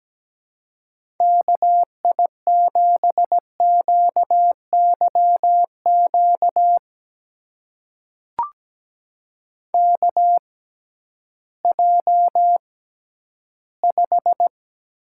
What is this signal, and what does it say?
Morse code, heard as a keyed tone, about 17 wpm, spaced as for 11 wpm, 700 Hz. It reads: KI7QYQ KJ5
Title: USA Call Signs - 6x Character Spacing - 17wpm